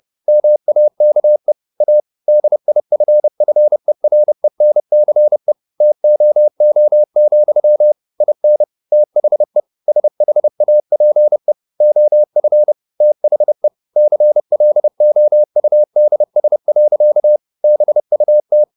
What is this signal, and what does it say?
Morse code, heard as a keyed tone, about 30 wpm, 600 Hz. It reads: MAKE A DIFFERENCE TOO, IN THE SHAPE OF THE CLOUDS. BUT
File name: the old man and the sea / the old man and the sea 1 30wpm